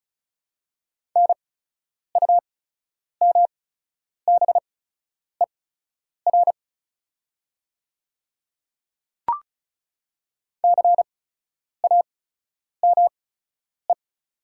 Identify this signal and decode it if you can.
Morse code, heard as a keyed tone, about 34 wpm, spaced as for 10 wpm, 690 Hz. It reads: NUMBER CAME
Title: 8x Character Spacing - Top 200 Words - 35wpm